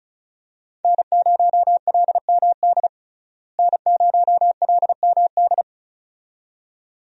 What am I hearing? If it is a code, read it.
Morse code, heard as a keyed tone, about 36 wpm, 700 Hz. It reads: N0LMD N0LMD
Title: USA Call Signs - 5 Characters - Repeated 2x - 35wpm